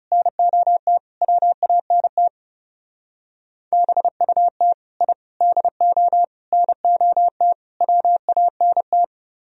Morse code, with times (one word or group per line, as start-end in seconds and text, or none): 0.1-1.0 NOT
1.2-2.3 WANT
3.7-4.7 BUT
5.0-5.1 I
5.4-6.2 DO
6.5-7.5 NOT
7.8-9.0 WANT